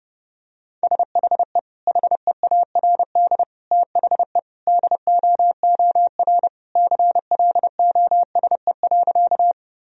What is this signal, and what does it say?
Morse code, heard as a keyed tone, about 30 wpm, 700 Hz. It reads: SHE HEARD THE DOOR CLOSE.
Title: Sentences from Top 1000 Words - Review Part 1 - 30wpm